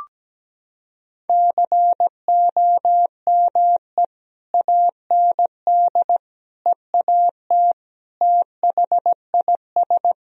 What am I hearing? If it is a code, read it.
Morse code, heard as a keyed tone, about 17 wpm, 700 Hz. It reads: COME AND EAT THIS
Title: Sentences from Top 1000 Words - Review Part 1 - 17wpm